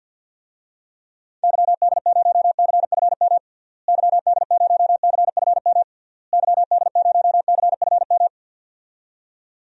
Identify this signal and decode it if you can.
Morse code, heard as a keyed tone, about 50 wpm, 700 Hz. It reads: YD0CFM YD0CFM YD0CFM
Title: DXCC-100 Call Signs - 6 Characters - Repeated 3x - 50wpm